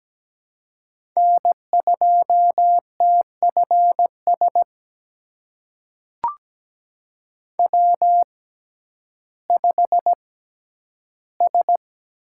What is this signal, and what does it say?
Morse code, heard as a keyed tone, about 17 wpm, spaced as for 10 wpm, 700 Hz. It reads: N2TFS W5S
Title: USA Call Signs - 5 Characters - 6x Character Spacing - 17wpm